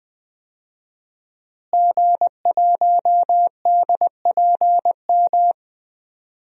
Morse code, heard as a keyed tone, about 20 wpm, 700 Hz. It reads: G1DPM
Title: DXCC-100 Call Signs - 5 Characters - Repeated 3x - 20wpm